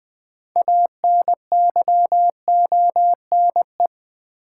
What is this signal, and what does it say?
Morse code, heard as a keyed tone, about 20 wpm, 700 Hz. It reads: ANYONE
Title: Sets of 5 Words Encore - New Word Drill - 20wpm